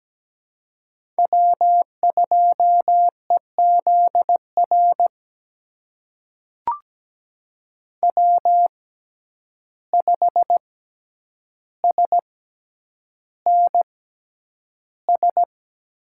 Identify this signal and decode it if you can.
Morse code, heard as a keyed tone, about 17 wpm, spaced as for 9 wpm, 700 Hz. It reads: W2EZR W5SNS
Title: USA Call Signs - 5 Characters - 6x Character Spacing - 17wpm